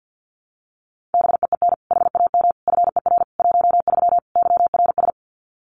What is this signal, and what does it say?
Morse code, heard as a keyed tone, about 50 wpm, 700 Hz. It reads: BEEN HAM FER 13 YRS
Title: Common QSO Phrases 50wpm